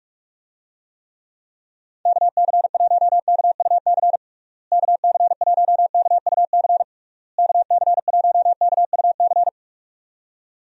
Morse code, heard as a keyed tone, about 46 wpm, 700 Hz. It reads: KC1KUC KC1KUC KC1KUC
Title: USA Call Signs - 6 Characters - Repeated 3x - 45wpm